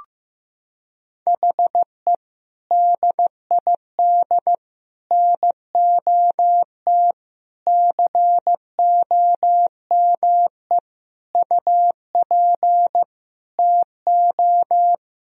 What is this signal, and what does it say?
Morse code, heard as a keyed tone, about 15 wpm, 700 Hz. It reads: HE DID NOT COME UP TO
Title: Sentences from Top 1000 Words - Review Part 3 - 15wpm